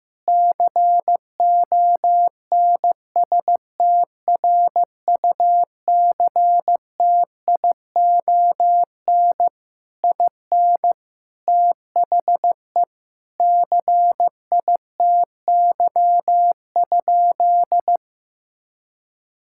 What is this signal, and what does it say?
Morse code, heard as a keyed tone, about 15 wpm, 700 Hz. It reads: CONSTRUCTION IN THE CITY?
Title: Sentences from Top 1600 Words 15wpm